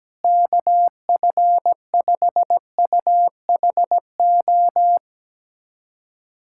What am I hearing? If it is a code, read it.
Morse code, heard as a keyed tone, about 17 wpm, 700 Hz. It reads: KF5UHO
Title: USA Callsign DE Callsign - 6 Characters - 17wpm